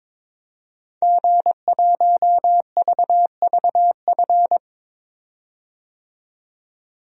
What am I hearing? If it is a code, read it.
Morse code, heard as a keyed tone, about 23 wpm, 700 Hz. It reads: G1VVF
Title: DXCC-100 Call Signs - 5 Characters - 22wpm